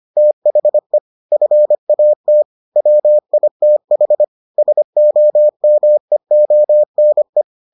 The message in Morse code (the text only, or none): THE FAT WITH SOMEONE